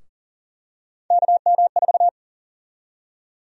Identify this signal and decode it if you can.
Morse code, heard as a keyed tone, about 40 wpm, 700 Hz. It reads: KM4
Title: USA Call Sign Prefixes - Repeated 2x - 40wpm